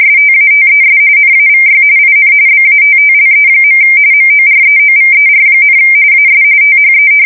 RTTY, about 45.45 baud. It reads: STATES NAVY IN BATTLE. MOREOVER, AS THE
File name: fsk170 / battleshipnj